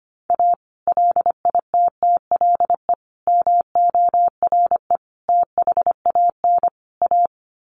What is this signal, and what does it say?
Morse code, heard as a keyed tone, about 25 wpm, 700 Hz. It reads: A LITTLE MORE THAN A
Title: Sets of 5 Words Encore2 25wpm